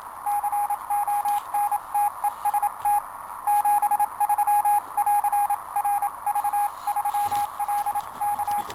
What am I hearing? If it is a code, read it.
Morse code, heard as a keyed tone, about 27 wpm, 830 Hz, with radio noise. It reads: CONTEST 73 <AR> RV3LK
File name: attachments / 10.07.2019 QSO 28mHz 1720 UT